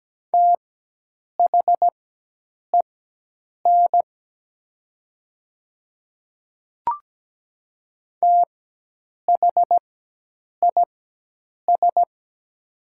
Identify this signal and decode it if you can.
Morse code, heard as a keyed tone, about 17 wpm, spaced as for 8 wpm, 700 Hz. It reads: THEN THIS
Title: 4x Character Spacing - Top 200 Words - 17wpm